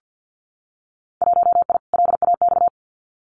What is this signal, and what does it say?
Morse code, heard as a keyed tone, about 50 wpm, 700 Hz. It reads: 1I LAX
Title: Winter Field Day Exchanges Encore 50wpm